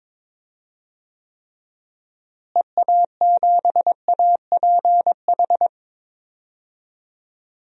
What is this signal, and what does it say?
Morse code, heard as a keyed tone, about 23 wpm, 700 Hz. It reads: EA7APH
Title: DXCC-100 Call Signs - 6 Characters - Repeated 3x - 22wpm